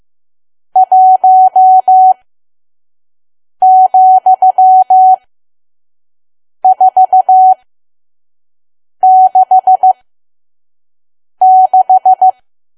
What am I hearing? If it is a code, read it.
Morse code, heard as a keyed tone, about 15 wpm, spaced as for 5 wpm, 750 Hz. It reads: 1,466
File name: Archive / 5 WPM files / PM